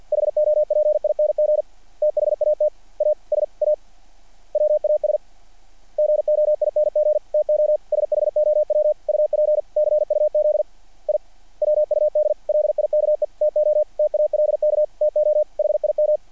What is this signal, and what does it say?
Morse code, heard as a keyed tone, about 50 wpm, 600 Hz. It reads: HOPING THAT ASA WAS GOING TO SHOW UP CUZ I WUD LIKE TO TALK TO HIM
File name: archived / CRSnet